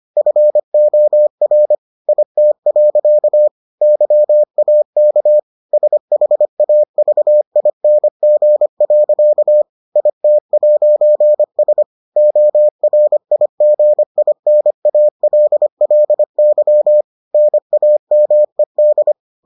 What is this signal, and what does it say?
Morse code, heard as a keyed tone, about 25 wpm, 600 Hz. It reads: FOR IT. YAK SHAVING. IT'S ORIGINALLY NAMED